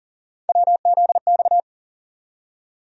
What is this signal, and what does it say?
Morse code, heard as a keyed tone, about 40 wpm, 700 Hz. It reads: WZX
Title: USA Call Sign Postfixes - 3 Letters 40wpm